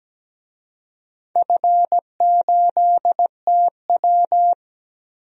Morse code, heard as a keyed tone, about 17 wpm, 700 Hz. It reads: F8TW
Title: DXCC-100 Call Signs - 4 Characters - 17wpm